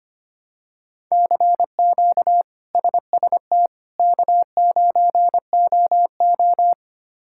Morse code, heard as a keyed tone, about 25 wpm, 700 Hz. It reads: CQ SST K9OO
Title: K1USN SST QSO Practice - 25wpm